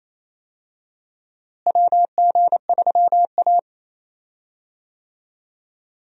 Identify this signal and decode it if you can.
Morse code, heard as a keyed tone, about 27 wpm, 700 Hz. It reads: WG3A
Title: USA Call Signs - 4 Characters - Repeated 3x - 28wpm